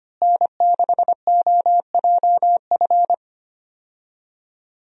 Morse code, heard as a keyed tone, about 25 wpm, 700 Hz. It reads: N6OJF
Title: USA Call Signs - 5 Characters - Repeated 2x - 25wpm